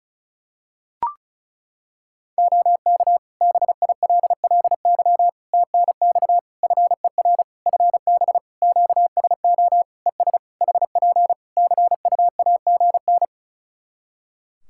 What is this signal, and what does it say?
Morse code, heard as a keyed tone, about 34 wpm, 700 Hz. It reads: OK BILLY TNX FER FB QSO ES HP CUAGN
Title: Practice QSO - Exchange 4 - Ending - 35wpm